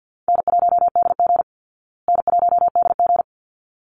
Morse code, heard as a keyed tone, about 48 wpm, 700 Hz. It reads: N1DG N1DG
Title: Call Signs Repeated 3x - 50wpm